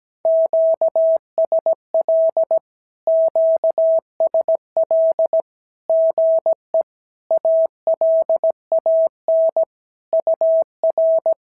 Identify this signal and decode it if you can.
Morse code, heard as a keyed tone, about 17 wpm, 650 Hz. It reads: QSL QSL GE ALAN UR
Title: SOTA QSO Practice 17wpm